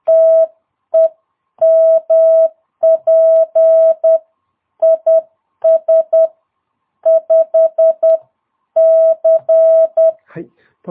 Morse code, heard as a keyed tone, about 10 wpm, 650 Hz. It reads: TEMP IS 5C
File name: jh8cbh / 2022cw